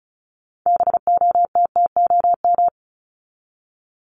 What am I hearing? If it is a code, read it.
Morse code, heard as a keyed tone, about 34 wpm, 700 Hz. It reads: BOTTOM
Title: Top 751-1000 Words 35wpm